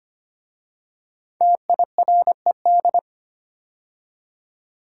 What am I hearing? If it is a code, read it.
Morse code, heard as a keyed tone, about 25 wpm, 700 Hz. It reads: TIRED
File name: 5 Letter Words - First 3 Letters Given - 25wpm